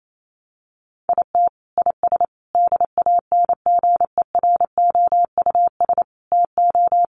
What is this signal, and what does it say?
Morse code, heard as a keyed tone, about 28 wpm, 700 Hz. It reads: IT IS DANGEROUS TO
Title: Sentences from Top 1200 Words 28wpm